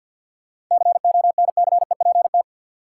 Morse code, heard as a keyed tone, about 52 wpm, 700 Hz. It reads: CONCEPT
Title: Sentences from Top 1100 Words - New Word Drill Alt - 50wpm